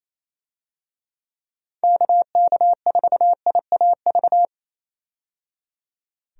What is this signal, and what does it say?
Morse code, heard as a keyed tone, about 28 wpm, 700 Hz. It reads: KK4IAV
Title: USA Call Signs - 6 Characters - 28wpm